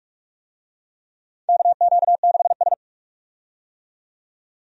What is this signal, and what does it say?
Morse code, heard as a keyed tone, about 41 wpm, 700 Hz. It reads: KQ6S
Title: USA Call Signs - 4 Characters - 45wpm